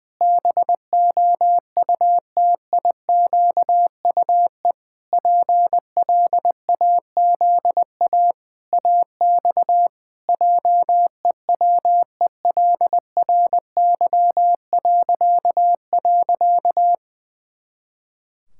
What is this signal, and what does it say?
Morse code, heard as a keyed tone, about 21 wpm, 700 Hz. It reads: BOUTIQUE PLAZA AX JEWELRY..
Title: Pangram Challenge v2 - 20wpm